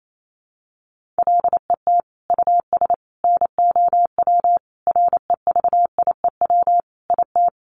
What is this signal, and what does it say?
Morse code, heard as a keyed tone, about 28 wpm, 700 Hz. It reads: LET US NOW REVIEW IT
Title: Sentences from Top 1500 Words 28wpm